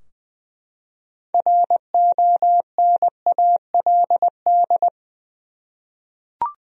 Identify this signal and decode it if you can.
Morse code, heard as a keyed tone, about 20 wpm, 700 Hz. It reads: RONALD
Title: USA Common Names - 20wpm